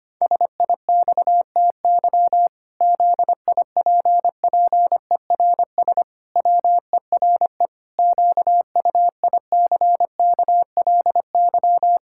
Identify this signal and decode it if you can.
Morse code, heard as a keyed tone, about 25 wpm, 700 Hz. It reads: SIXTY ZIPPERS WERE QUICKLY
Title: Pangram Challenge v2 - 25wpm